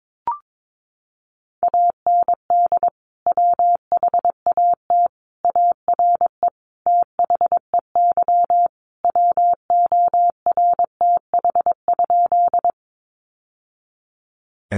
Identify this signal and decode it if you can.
Morse code, heard as a keyed tone, about 22 wpm, 700 Hz. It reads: AND WHAT ARE THEY WORTH?
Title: Sentences from Top 800 Words 22wpm